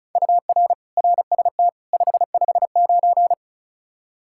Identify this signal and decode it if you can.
Morse code, heard as a keyed tone, about 34 wpm, 700 Hz. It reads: UR RST 559
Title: Practice QSO - Exchange 1 - Beginning - 35wpm